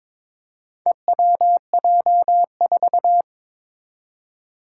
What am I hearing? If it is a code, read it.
Morse code, heard as a keyed tone, about 22 wpm, 700 Hz. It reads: EWJ4
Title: Four Challenging Letter-Number Combinations 22wpm